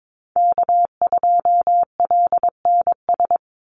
Morse code, heard as a keyed tone, about 22 wpm, 700 Hz. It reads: K2LNS
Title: Call Signs Repeated 3x - 22wpm